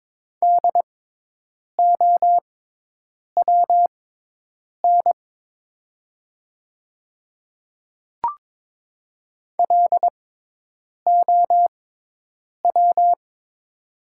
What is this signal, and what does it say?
Morse code, heard as a keyed tone, about 22 wpm, spaced as for 8 wpm, 700 Hz. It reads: DOWN LOW
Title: 6x Character Spacing - Top 300 Words - 22wpm